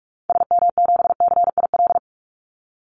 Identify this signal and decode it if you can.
Morse code, heard as a keyed tone, about 44 wpm, 700 Hz. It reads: SM7CIL
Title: Call Signs - 45wpm